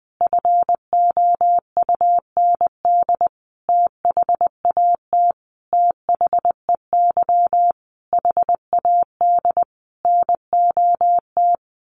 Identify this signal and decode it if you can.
Morse code, heard as a keyed tone, about 20 wpm, 700 Hz. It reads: FOUND THAT THEY HAD NOT
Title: Sentences from Top 1200 Words 20wpm